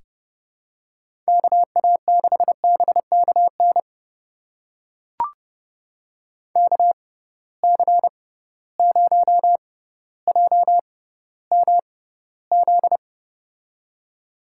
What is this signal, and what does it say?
Morse code, heard as a keyed tone, about 30 wpm, spaced as for 15 wpm, 700 Hz. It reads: KA6BKN KC0JMZ
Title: USA Call Signs - 6 Characters - 6x Character Spacing - 30wpm